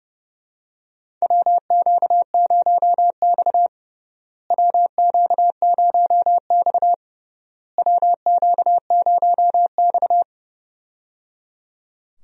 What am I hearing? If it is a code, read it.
Morse code, heard as a keyed tone, about 30 wpm, 700 Hz. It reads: WQ0X WQ0X WQ0X
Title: USA Call Signs - 4 Characters - Repeated 3x - 30wpm